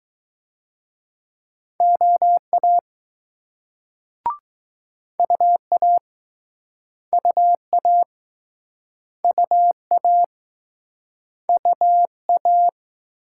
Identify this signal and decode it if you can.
Morse code, heard as a keyed tone, about 20 wpm, 700 Hz. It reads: OA UA UA UA UA